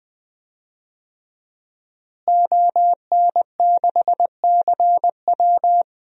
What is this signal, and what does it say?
Morse code, heard as a keyed tone, about 20 wpm, 700 Hz. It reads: ON6CW
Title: DXCC-100 Call Signs - 5 Characters - Repeated 3x - 20wpm